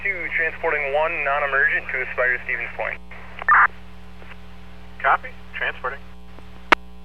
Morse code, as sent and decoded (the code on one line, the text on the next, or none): none